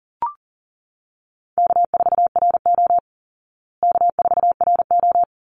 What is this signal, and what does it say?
Morse code, heard as a keyed tone, about 37 wpm, 700 Hz. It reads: K4RO K4RO
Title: Easy Call Signs Repeated 2x - 40wpm